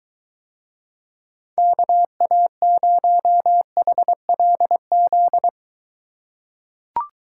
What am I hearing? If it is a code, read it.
Morse code, heard as a keyed tone, about 23 wpm, 700 Hz. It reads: KA0HLZ